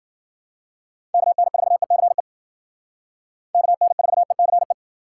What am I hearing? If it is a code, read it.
Morse code, heard as a keyed tone, about 53 wpm, 700 Hz. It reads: KN4ECE KN4ECE